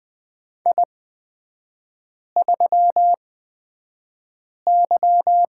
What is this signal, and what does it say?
Morse code, heard as a keyed tone, about 20 wpm, spaced as for 5 wpm, 700 Hz. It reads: I3Y
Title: 250 Lesson 40e - BK - Three Characters 20wpm